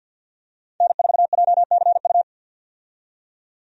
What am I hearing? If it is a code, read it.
Morse code, heard as a keyed tone, about 52 wpm, 700 Hz. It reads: N4JCU
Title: USA Call Signs - 5 Characters - 50wpm